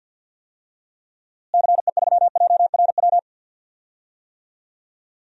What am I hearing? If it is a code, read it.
Morse code, heard as a keyed tone, about 49 wpm, 700 Hz. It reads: CE3JRW
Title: DXCC-100 Call Signs - 6 Characters - Repeated 2x - 50wpm